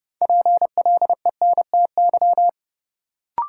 Morse code, heard as a keyed tone, about 32 wpm, 700 Hz. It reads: PLENTY
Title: Sentences from Top 1300 Words - New Word Drill - 30wpm